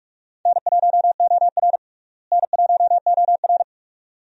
Morse code, heard as a keyed tone, about 46 wpm, 700 Hz. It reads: N1OR N1OR
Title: USA Call Signs - 4 Characters - Repeated 2x - 45wpm